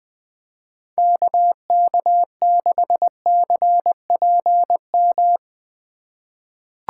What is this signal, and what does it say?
Morse code, heard as a keyed tone, about 20 wpm, 700 Hz. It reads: KK6CPM
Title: USA Call Signs - 6 Characters - 4x Character Spacing - 20wpm